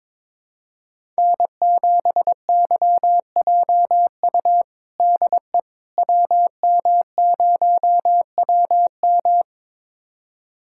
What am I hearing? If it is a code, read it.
Morse code, heard as a keyed tone, about 22 wpm, 700 Hz. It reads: N7YJU DE WM0WM
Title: USA Callsign DE Callsign - 5 Characters - 22wpm